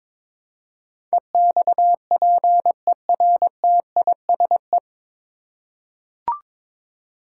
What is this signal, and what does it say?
Morse code, heard as a keyed tone, about 22 wpm, 700 Hz. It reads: EXPERTISE
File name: 9 Letter Words - First 3 Letters Given - 22wpm